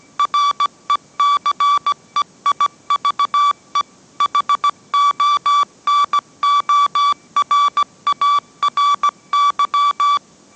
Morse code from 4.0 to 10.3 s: HONORARY